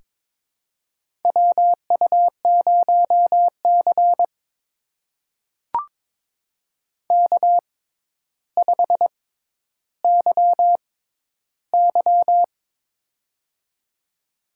Morse code, heard as a keyed tone, about 23 wpm, spaced as for 11 wpm, 700 Hz. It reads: WU0C K5YY
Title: USA Call Signs - 4 Characters - 6x Character Spacing - 22wpm